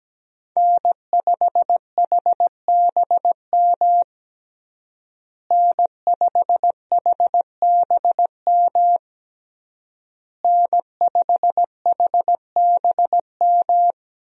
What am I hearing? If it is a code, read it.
Morse code, heard as a keyed tone, about 17 wpm, 700 Hz. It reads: N5HBM N5HBM N5HBM
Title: USA Call Signs - 5 Characters - Repeated 3x - 17wpm